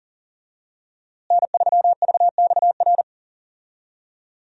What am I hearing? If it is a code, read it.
Morse code, heard as a keyed tone, about 37 wpm, 700 Hz. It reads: N3VXR
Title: USA Call Signs - 5 Characters - Repeated 3x - 40wpm